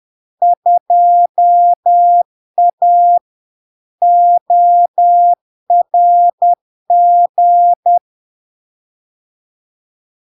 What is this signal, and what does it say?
Morse code, heard as a keyed tone, about 10 wpm, 700 Hz. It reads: 2A ORG
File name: LARK-FD-2022-Log-in-CW-10wpm-700-Hz